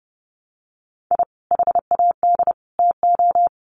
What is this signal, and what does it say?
Morse code, heard as a keyed tone, about 30 wpm, 700 Hz. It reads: I HAD TO
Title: Sets of 3 Words 30wpm